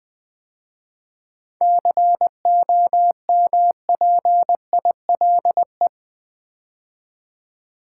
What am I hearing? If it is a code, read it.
Morse code, heard as a keyed tone, about 20 wpm, 700 Hz. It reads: COMPILE
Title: 7 Letter Words - First 3 Letters Given - 20wpm